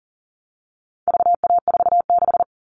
Morse code, heard as a keyed tone, about 40 wpm, 700 Hz. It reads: VA46
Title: All Cut Numbers - 4-Digit 40wpm